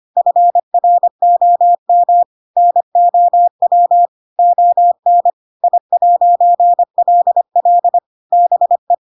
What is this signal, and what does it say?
Morse code, heard as a keyed tone, about 25 wpm, 700 Hz. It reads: FROM NOW ON I'LL BE
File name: Pinocchio / Pinocchio ch8 25 25 ik1hge